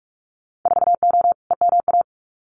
Morse code, heard as a keyed tone, about 44 wpm, 700 Hz. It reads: 4O EMA
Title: Winter Field Day Exchanges 45wpm